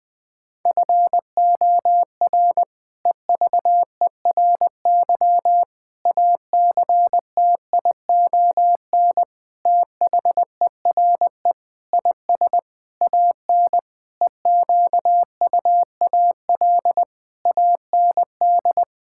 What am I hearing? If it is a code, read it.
Morse code, heard as a keyed tone, about 20 wpm, 700 Hz. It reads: FOR EVERY ACTION THERE IS AN EQUAL AND
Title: Sentences with 8-Word Phrases - 20wpm